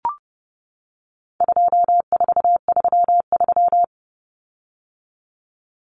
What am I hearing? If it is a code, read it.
Morse code, heard as a keyed tone, about 28 wpm, 700 Hz. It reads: 2433
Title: Common Cut Numbers - 4-Digit 30wpm